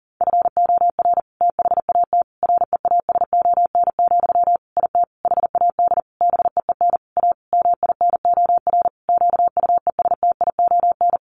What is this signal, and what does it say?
Morse code, heard as a keyed tone, about 40 wpm, 700 Hz. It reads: FOR THAT REASON, IT HAD BEEN A MINOR QUESTION
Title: Sentences from Top 1500 Words 40wpm